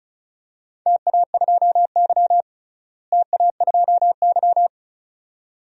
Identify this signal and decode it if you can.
Morse code, heard as a keyed tone, about 34 wpm, 700 Hz. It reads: TA2Y TA2Y
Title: DXCC-100 Call Signs - 4 Characters - Repeated 2x - 35wpm